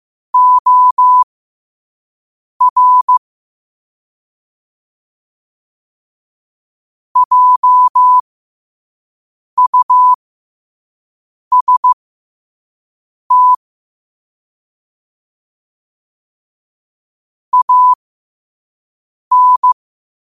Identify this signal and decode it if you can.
Morse code, heard as a keyed tone, about 15 wpm, spaced as for 5 wpm, 1000 Hz. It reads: OR JUST AN
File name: qotdfiles / 5wpm